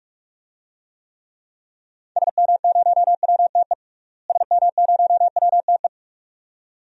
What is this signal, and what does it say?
Morse code, heard as a keyed tone, about 46 wpm, 700 Hz. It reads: SM0WTE SM0WTE
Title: DXCC-100 Call Signs - 6 Characters - Repeated 2x - 45wpm